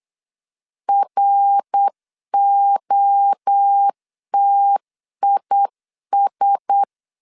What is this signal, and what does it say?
Morse code, heard as a keyed tone, about 8 wpm, 800 Hz. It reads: ROTIS
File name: CW / DET / 1 6 1500C S C 8wpm 5219